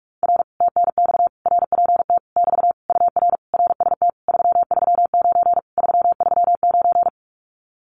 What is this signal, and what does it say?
Morse code, heard as a keyed tone, about 46 wpm, 700 Hz. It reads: R TNX RPT <BT> UR RST 339 339
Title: Common QSO Phrases Encore 45wpm